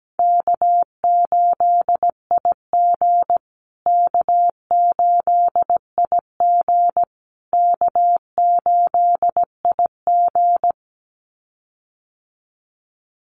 Morse code, heard as a keyed tone, about 17 wpm, 700 Hz. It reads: K8IG K8IG K8IG
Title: USA Call Signs Repeated 3x 17wpm